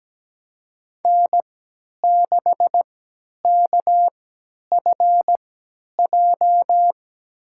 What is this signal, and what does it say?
Morse code, heard as a keyed tone, about 17 wpm, spaced as for 10 wpm, 700 Hz. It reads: N6KFJ
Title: USA Call Signs - 5 Characters - 3x Character Spacing - 17wpm